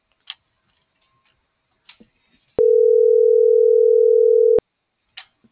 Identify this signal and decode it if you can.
Ringback tone (North America), 440 + 480 Hz.
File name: CENTRAL MORENA / out ruth.skyangel.24 - .561